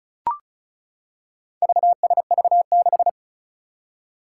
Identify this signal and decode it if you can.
Morse code, heard as a keyed tone, about 36 wpm, 700 Hz. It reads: VSV6
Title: Four Challenging Letter-Number Combinations 35wpm